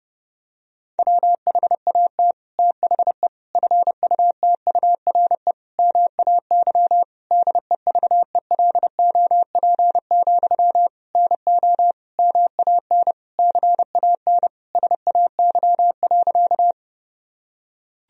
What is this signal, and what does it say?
Morse code, heard as a keyed tone, about 30 wpm, 700 Hz. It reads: WHAT THE FUTURE MAY DEVELOP, NO MAN CAN SAY.
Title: Sentences from Top 1000 Words - Review Part 2 - 30wpm